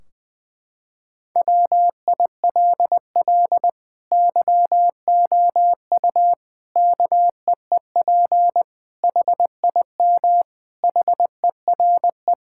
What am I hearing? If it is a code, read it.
Morse code, heard as a keyed tone, about 20 wpm, 700 Hz. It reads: WILL YOU KEEP HIM HERE
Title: Sentences from Top 1600 Words 20wpm